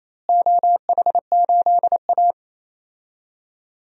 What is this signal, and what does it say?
Morse code, heard as a keyed tone, about 28 wpm, 700 Hz. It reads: OH8A
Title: DXCC-100 Call Signs - 4 Characters - Repeated 3x - 28wpm